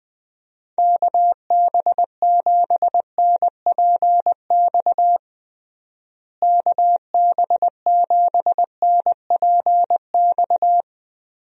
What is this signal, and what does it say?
Morse code, heard as a keyed tone, about 20 wpm, 700 Hz. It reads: KB7NPX KB7NPX
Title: USA Call Signs - 6 Characters - Repeated 3x - 20wpm